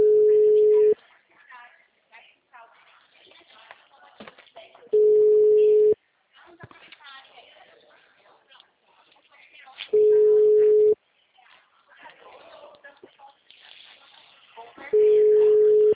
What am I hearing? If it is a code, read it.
Ringback tone (Europe, 425 Hz), with speech.
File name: out- -2018- - - .44